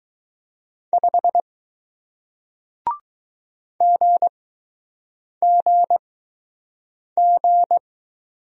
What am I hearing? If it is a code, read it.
Morse code, heard as a keyed tone, about 20 wpm, 700 Hz. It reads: 5 G G G